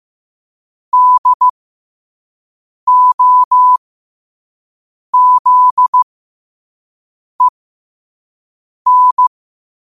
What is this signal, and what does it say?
Morse code, heard as a keyed tone, about 15 wpm, spaced as for 5 wpm, 1000 Hz. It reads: DOZEN
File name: qotdfiles / 5wpm